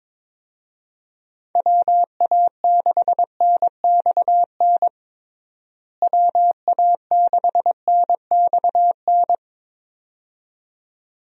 Morse code, heard as a keyed tone, about 22 wpm, 700 Hz. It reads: WA6NXN WA6NXN
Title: USA Call Signs - 6 Characters - Repeated 2x - 22wpm